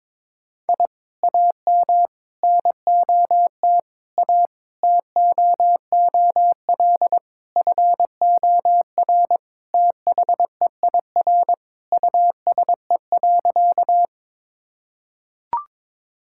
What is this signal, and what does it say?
Morse code, heard as a keyed tone, about 22 wpm, 700 Hz. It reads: I AM NOT A TOOL FOR THEIR USE.
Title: Sentences from Top 1000 Words - Review Part 6 - 22wpm